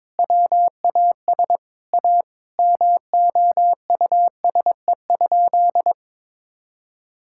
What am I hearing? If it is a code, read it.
Morse code, heard as a keyed tone, about 22 wpm, 700 Hz. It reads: WAS A MOUSE?
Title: Sentences from Top 1000 Words - Review Part 5 - 22wpm